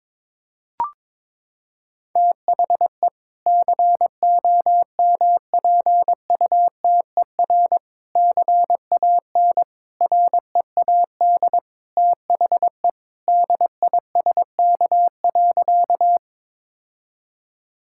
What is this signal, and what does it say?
Morse code, heard as a keyed tone, about 22 wpm, 700 Hz. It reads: THE COMPUTER CAN READ THE DISK.
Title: Sentences from Top 1000 Words - Review Part 6 - 22wpm